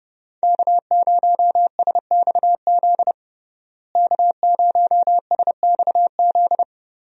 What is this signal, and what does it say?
Morse code, heard as a keyed tone, about 30 wpm, 700 Hz. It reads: K0SXZ K0SXZ
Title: USA Call Signs - 5 Characters - Repeated 3x - 30wpm